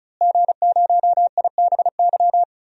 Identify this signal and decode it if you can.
Morse code, heard as a keyed tone, about 36 wpm, 700 Hz. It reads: G0IBY
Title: DXCC-100 Call Signs - 5 Characters - Repeated 2x - 35wpm